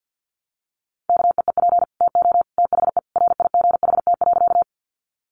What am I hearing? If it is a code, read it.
Morse code, heard as a keyed tone, about 48 wpm, 700 Hz. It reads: KEEP TO THE RIGHT.
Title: Sentences from Top 300 Words 50wpm